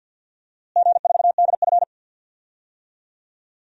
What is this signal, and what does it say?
Morse code, heard as a keyed tone, about 50 wpm, 700 Hz. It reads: G4DF
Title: DXCC-100 Call Signs - 4 Characters - Repeated 3x - 50wpm